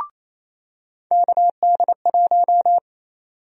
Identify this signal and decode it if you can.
Morse code, heard as a keyed tone, about 27 wpm, 700 Hz. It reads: KD1
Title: USA Call Sign Prefixes - 28wpm